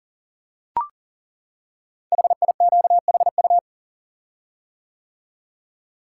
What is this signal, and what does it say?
Morse code, heard as a keyed tone, about 40 wpm, 690 Hz. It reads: HIQHU
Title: Code Groups - 5 Letters - 40wpm